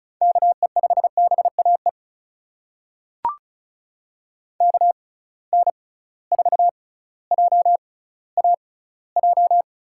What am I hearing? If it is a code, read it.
Morse code, heard as a keyed tone, about 34 wpm, spaced as for 18 wpm, 700 Hz. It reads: KE5BAE KN4JAJ
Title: USA Call Signs - 6 Characters - 6x Character Spacing - 35wpm